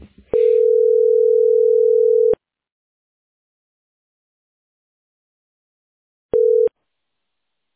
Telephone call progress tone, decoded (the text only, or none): ringback tone